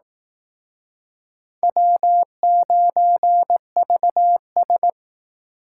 Morse code, heard as a keyed tone, about 18 wpm, 700 Hz. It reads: W9VS
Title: Speed Racing - USA 1x2 Call Signs 15wpm